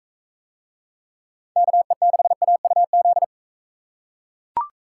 Morse code, heard as a keyed tone, about 41 wpm, 700 Hz. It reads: KE6AUZ